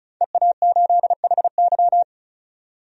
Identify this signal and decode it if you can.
Morse code, heard as a keyed tone, about 34 wpm, 700 Hz. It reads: EA8HY
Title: DXCC-100 Call Signs - 5 Characters - 35wpm